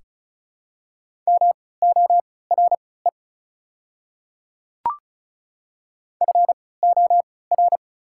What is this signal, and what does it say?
Morse code, heard as a keyed tone, about 36 wpm, spaced as for 20 wpm, 690 Hz. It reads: MORE FOR
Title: 3x Character Spacing - Top 100 Words - 35wpm